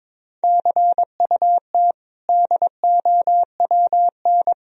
Morse code, heard as a keyed tone, about 22 wpm, 700 Hz. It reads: CUT DOWN
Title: Sentences from Top 1000 Words - Review Part 1 - 22wpm